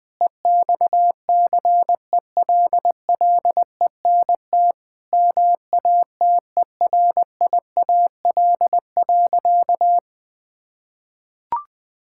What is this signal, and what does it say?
Morse code, heard as a keyed tone, about 21 wpm, 700 Hz. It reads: EXCELLENT MATERIAL.
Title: Sentences from Top 1000 Words - Review Part 5 - 20wpm